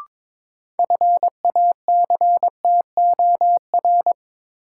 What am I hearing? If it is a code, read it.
Morse code, heard as a keyed tone, about 22 wpm, 700 Hz. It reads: FACTOR
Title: Sentences from Top 1300 Words - New Word Drill Alt - 22wpm